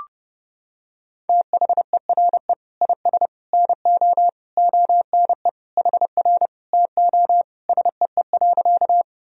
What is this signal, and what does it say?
Morse code, heard as a keyed tone, about 32 wpm, 700 Hz. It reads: THERE IS NO ONE HR TO SEE.
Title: Sentences from Top 200 Words and Abbreviations 30wpm